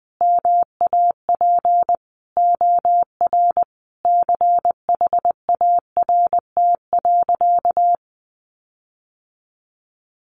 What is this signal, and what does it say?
Morse code, heard as a keyed tone, about 19 wpm, 700 Hz. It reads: MAP OR CHART.
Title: Sentences from Top 1300 Words 20wpm